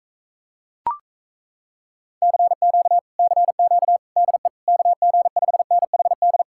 Morse code, heard as a keyed tone, about 37 wpm, 700 Hz. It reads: CQ CQ DE KG5NHD